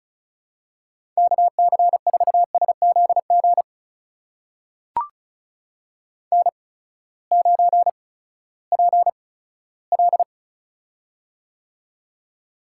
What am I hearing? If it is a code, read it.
Morse code, heard as a keyed tone, about 36 wpm, spaced as for 18 wpm, 700 Hz. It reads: KC4SZG N9PL
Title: USA Call Signs - 8x Character Spacing - 35wpm